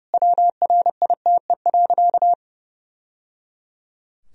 Morse code, heard as a keyed tone, about 30 wpm, 700 Hz. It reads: WRITE.
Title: Sentences from Top 1000 Words - Review Part 1 - 30wpm